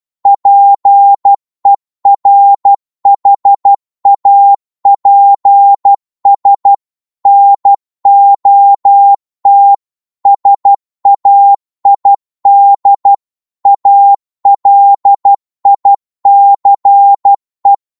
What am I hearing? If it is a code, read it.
Morse code, heard as a keyed tone, about 12 wpm, 800 Hz. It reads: PERHAPS NOT SAID ALICE